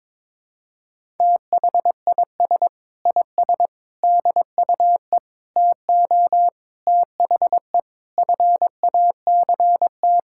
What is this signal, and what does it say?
Morse code, heard as a keyed tone, about 22 wpm, 700 Hz. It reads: THIS IS DUE TO THE FACT
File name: practice / 22wpm / Sets of 8 Words - 22wpm